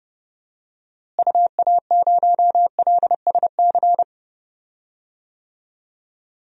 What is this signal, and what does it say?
Morse code, heard as a keyed tone, about 30 wpm, 700 Hz. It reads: UA0LSC
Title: DXCC-100 Call Signs - 6 Characters - Repeated 3x - 30wpm